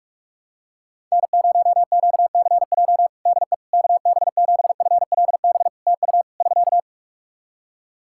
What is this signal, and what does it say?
Morse code, heard as a keyed tone, about 46 wpm, 700 Hz. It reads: N0QCJ DE KB7FLB TU <SK>
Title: Practice QSO - Exchange 4 - Ending - 45wpm